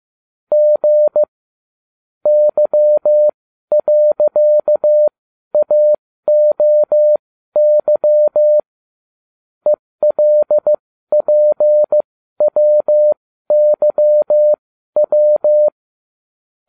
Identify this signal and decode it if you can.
Morse code, heard as a keyed tone, about 15 wpm, spaced as for 12 wpm, 600 Hz. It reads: G Y.AOY ELPWYW